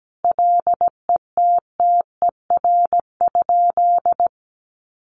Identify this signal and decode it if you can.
Morse code, heard as a keyed tone, about 17 wpm, 700 Hz. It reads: LETTER?
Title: Sentences from Top 300 Words 17wpm